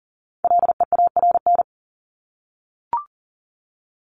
Morse code, heard as a keyed tone, about 40 wpm, 700 Hz. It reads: LEARN
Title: ING Suffix - Top 100 Words with Base Words 40wpm